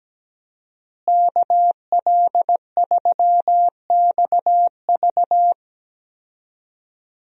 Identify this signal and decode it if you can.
Morse code, heard as a keyed tone, about 17 wpm, 700 Hz. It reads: KL3XV
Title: USA Call Signs - 5 Characters - Repeated 3x - 17wpm